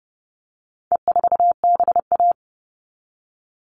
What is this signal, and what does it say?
Morse code, heard as a keyed tone, about 30 wpm, 700 Hz. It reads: E4BA
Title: All Cut Numbers - 4-Digit 30wpm